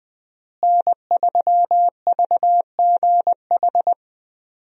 Morse code, heard as a keyed tone, about 20 wpm, 700 Hz. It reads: N3VGH
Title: USA Call Signs - 5 Characters - Repeated 3x - 20wpm